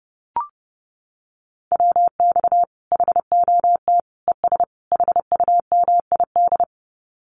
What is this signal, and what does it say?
Morse code, heard as a keyed tone, about 30 wpm, 700 Hz. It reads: WX HOT ES HUMID
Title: Common QSO Phrases 30wpm